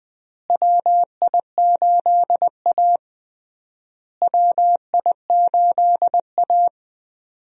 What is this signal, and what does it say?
Morse code, heard as a keyed tone, about 20 wpm, 700 Hz. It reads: WI8A WI8A
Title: USA Call Signs - 4 Characters - Repeated 3x - 20wpm